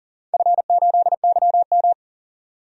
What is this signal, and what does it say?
Morse code, heard as a keyed tone, about 39 wpm, 700 Hz. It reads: F8YM
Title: DXCC-100 Call Signs - 4 Characters - 40wpm